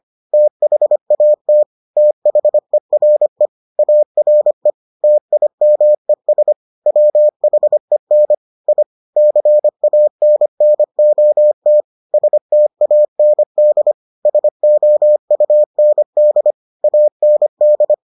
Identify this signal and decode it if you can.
Morse code, heard as a keyed tone, about 25 wpm, 600 Hz. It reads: THAT THERE ARE TIMES WHEN I CANNOT STAND SOUND AND